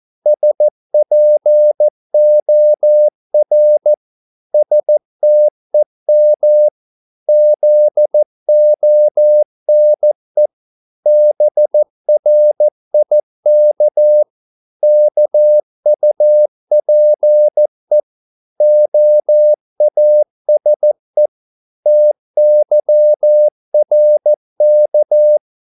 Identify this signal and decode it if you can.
Morse code, heard as a keyed tone, about 14 wpm, 600 Hz. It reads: SPOR STEM ZONE BRIK KUPE OASE TYRK
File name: Korte ord DK 14wpm